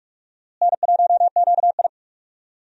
Morse code, heard as a keyed tone, about 46 wpm, 700 Hz. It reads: N1QI
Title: USA Call Signs - 4 Characters - Repeated 3x - 45wpm